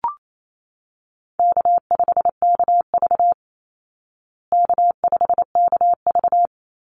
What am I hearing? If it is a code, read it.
Morse code, heard as a keyed tone, about 28 wpm, 700 Hz. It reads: K5KV K5KV
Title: Easy Call Signs Repeated 3x - 28wpm